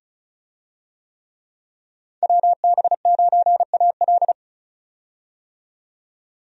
Morse code, heard as a keyed tone, about 36 wpm, 700 Hz. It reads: WB9AL
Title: USA Call Signs - 5 Characters - 35wpm